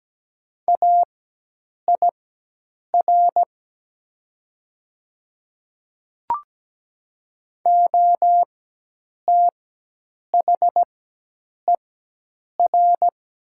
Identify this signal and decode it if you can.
Morse code, heard as a keyed tone, about 17 wpm, spaced as for 8 wpm, 690 Hz. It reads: AIR OTHER
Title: 4x Character Spacing - Top 300 Words - 17wpm